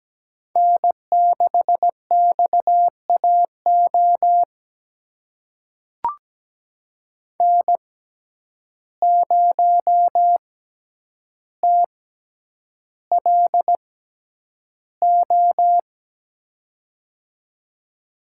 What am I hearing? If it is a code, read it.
Morse code, heard as a keyed tone, about 17 wpm, spaced as for 9 wpm, 700 Hz. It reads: N6XAO N0TLO
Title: USA Call Signs - 5 Characters - 6x Character Spacing - 17wpm